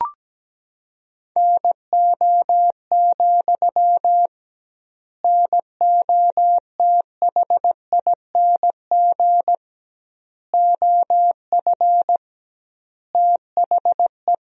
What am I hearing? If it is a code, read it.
Morse code, heard as a keyed tone, about 17 wpm, 700 Hz. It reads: NO, NOTHING OF THE
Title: 2x Word-Spacing - Sentences from Top 500 Words - 17wpm